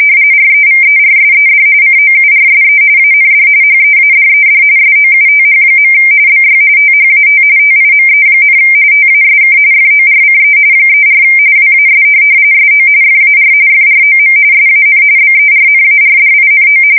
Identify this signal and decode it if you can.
RTTY, about 45.45 baud. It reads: REVERSED COURSE AND HEADED SOUTH. BY THAT TIME, TASK FORCE 34 WAS SO FAR FROM SAN BERNARDINO
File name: fsk170 / USS-New-Jersey